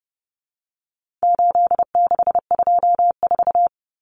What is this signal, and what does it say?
Morse code, heard as a keyed tone, about 30 wpm, 700 Hz. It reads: 8624
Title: Common Cut Numbers - 4-Digit 30wpm